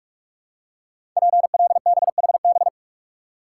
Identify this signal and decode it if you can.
Morse code, heard as a keyed tone, about 44 wpm, 700 Hz. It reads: PLBHB
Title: Code Groups - 5 Letter:Numbers - 45wpm